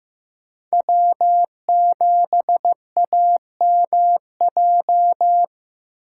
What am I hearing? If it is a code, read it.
Morse code, heard as a keyed tone, about 15 wpm, 700 Hz. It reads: W7AMJ
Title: USA Call Signs - 5 Characters - 6x Character Spacing - 15wpm